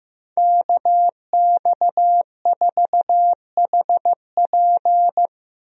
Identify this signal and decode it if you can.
Morse code, heard as a keyed tone, about 15 wpm, 700 Hz. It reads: KX4HP
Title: USA Call Signs - 5 Characters - Repeated 3x - 15wpm